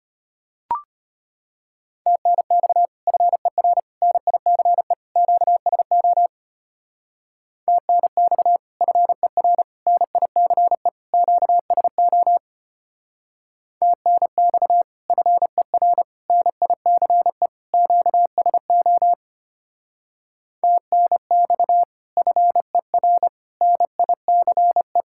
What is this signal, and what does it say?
Morse code, heard as a keyed tone, about 32 wpm, 700 Hz. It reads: TNX FER NICE QSO TNX FER NICE QSO TNX FER NICE QSO TNX FER NICE